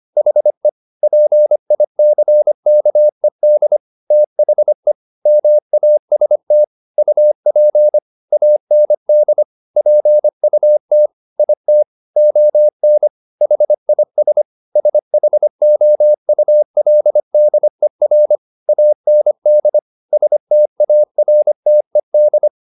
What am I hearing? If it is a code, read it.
Morse code, heard as a keyed tone, about 25 wpm, 600 Hz. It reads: HE PICKED THE MAST UP AND PUT IT ON HIS SHOULDER AND STARTED